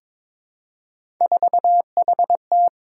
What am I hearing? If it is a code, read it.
Morse code, heard as a keyed tone, about 23 wpm, 700 Hz. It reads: 4HT
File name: practice / 22wpm / Three Letter-Number Combinations - 22wpm